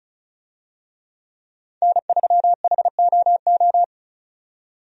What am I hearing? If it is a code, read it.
Morse code, heard as a keyed tone, about 34 wpm, 700 Hz. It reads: N3HOO
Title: USA Call Signs - 5 Characters - 35wpm